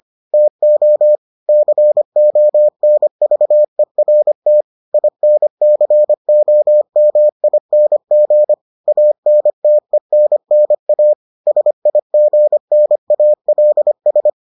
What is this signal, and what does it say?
Morse code, heard as a keyed tone, about 25 wpm, 600 Hz. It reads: TO CONVERT INCOMING ANTENNA SIGNALS